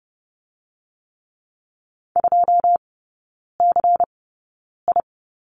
Morse code, heard as a keyed tone, about 30 wpm, spaced as for 9 wpm, 700 Hz. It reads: 2CI
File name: Lesson 16e - Letter c - Three Characters 30wpm